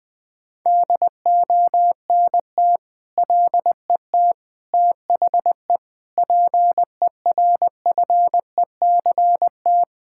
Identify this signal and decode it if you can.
Morse code, heard as a keyed tone, about 21 wpm, 700 Hz. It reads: DONT LET THE PERFECT
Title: Sentences with 9-Word Phrases - 20wpm